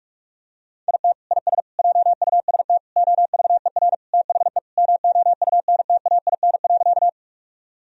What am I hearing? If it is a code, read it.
Morse code, heard as a keyed tone, about 46 wpm, 700 Hz. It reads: IT IS JUST OVER THE MOUNTAIN.
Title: Sentences from Top 1000 Words - Review Part 1 - 45wpm